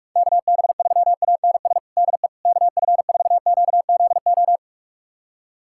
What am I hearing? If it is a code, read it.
Morse code, heard as a keyed tone, about 46 wpm, 700 Hz. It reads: KB3ANS DE KF4QZO
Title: USA Callsign DE Callsign - 6 Characters - 45wpm